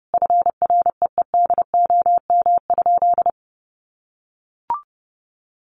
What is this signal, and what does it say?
Morse code, heard as a keyed tone, about 30 wpm, 700 Hz. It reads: FREEDOM?
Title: Sentences from Top 1500 Words 30wpm